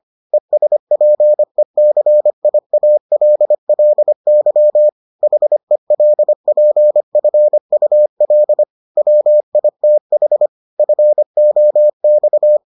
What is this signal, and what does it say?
Morse code, heard as a keyed tone, about 25 wpm, 600 Hz. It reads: ESPECIALLY HELPFUL WITH FOX